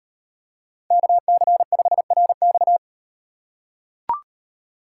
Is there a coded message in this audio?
Yes: Morse code from 0.9 to 2.8 s, heard as a keyed tone, about 38 wpm, 700 Hz. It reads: KC5RX